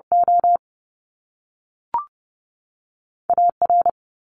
Morse code, heard as a keyed tone, about 30 wpm, spaced as for 3 wpm, 700 Hz. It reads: OA R
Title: Top Two Letter Combinations in Words 30wpm